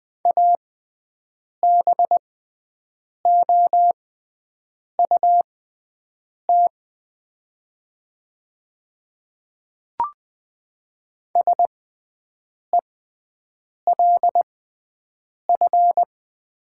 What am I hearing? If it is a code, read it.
Morse code, heard as a keyed tone, about 20 wpm, spaced as for 7 wpm, 700 Hz. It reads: ABOUT SELF
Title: 6x Character Spacing - Top 500 Words - 20wpm